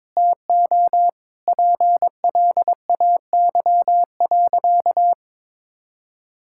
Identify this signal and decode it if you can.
Morse code, heard as a keyed tone, about 22 wpm, 700 Hz. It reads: TO PLAY.
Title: Sentences from Top 1000 Words - Review Part 6 - 22wpm